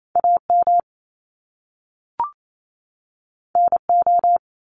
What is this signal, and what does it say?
Morse code, heard as a keyed tone, about 28 wpm, 700 Hz. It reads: AM NO
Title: Top Two Letter Combinations in Words 28wpm